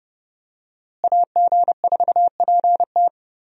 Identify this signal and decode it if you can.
Morse code, heard as a keyed tone, about 30 wpm, 700 Hz. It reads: AG4PT
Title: USA Call Signs - 5 Characters - Repeated 2x - 30wpm